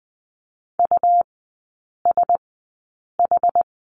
Morse code, heard as a keyed tone, about 20 wpm, spaced as for 8 wpm, 690 Hz. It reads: USH
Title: 212 Lesson 34e - Number 8 - Three Characters 20wpm